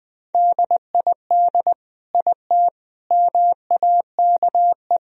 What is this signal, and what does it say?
Morse code, heard as a keyed tone, about 20 wpm, 700 Hz. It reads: DID IT MAKE
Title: Sentences from Top 1000 Words - Review Part 4 - 20wpm